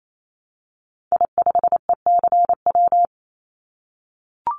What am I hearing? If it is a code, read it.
Morse code, heard as a keyed tone, about 28 wpm, 700 Hz. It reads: I5ECW
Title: Call Signs - 28wpm